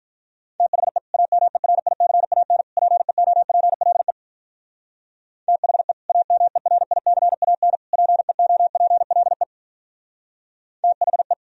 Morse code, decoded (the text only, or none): THE AMERICAN PEOPLE THE AMERICAN PEOPLE THE